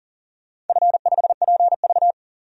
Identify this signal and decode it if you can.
Morse code, heard as a keyed tone, about 40 wpm, 700 Hz. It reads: F5PV
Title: DXCC-100 Call Signs - 4 Characters - 40wpm